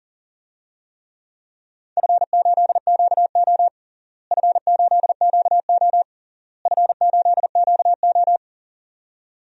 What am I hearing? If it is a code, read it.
Morse code, heard as a keyed tone, about 40 wpm, 700 Hz. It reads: F8QO F8QO F8QO
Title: DXCC-100 Call Signs - 4 Characters - Repeated 3x - 40wpm